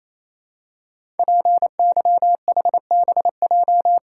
Morse code, heard as a keyed tone, about 28 wpm, 700 Hz. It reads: PYHBJ
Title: Code Groups - 5 Letters - 28wpm